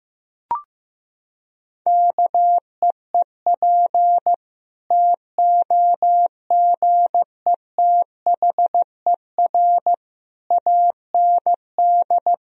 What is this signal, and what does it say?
Morse code, heard as a keyed tone, about 15 wpm, 700 Hz. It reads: KEEP TOGETHER AND
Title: Sentences from Top 1000 Words - Review Part 1 - 15wpm